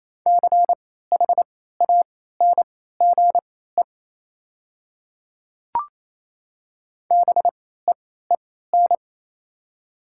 Morse code, heard as a keyed tone, about 27 wpm, spaced as for 16 wpm, 700 Hz. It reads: CHANGE BEEN
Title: 3x Character Spacing - Top 300 Words - 28wpm